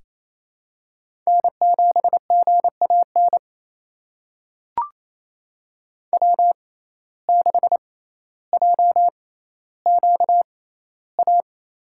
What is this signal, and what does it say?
Morse code, heard as a keyed tone, about 27 wpm, spaced as for 14 wpm, 700 Hz. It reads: N7GAN W6JQA
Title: USA Call Signs - 5 Characters - 6x Character Spacing - 28wpm